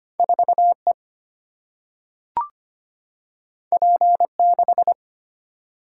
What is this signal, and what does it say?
Morse code, heard as a keyed tone, about 25 wpm, 690 Hz. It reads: 4E P6
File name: Two Letter-Number Combinations - 25wpm